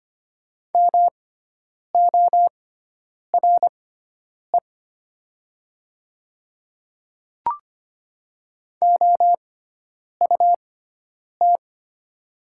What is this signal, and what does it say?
Morse code, heard as a keyed tone, about 25 wpm, spaced as for 9 wpm, 690 Hz. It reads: MORE OUT
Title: 6x Character Spacing - Top 100 Words - 25wpm